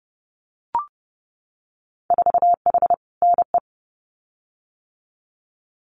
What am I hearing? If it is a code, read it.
Morse code, heard as a keyed tone, about 30 wpm, 700 Hz. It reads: 4H NE
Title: Winter Field Day Exchanges Encore 30wpm